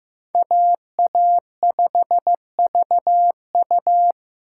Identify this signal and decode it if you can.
Morse code, heard as a keyed tone, about 15 wpm, 700 Hz. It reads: AA5VU
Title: K1USN SST QSO Practice - 15wpm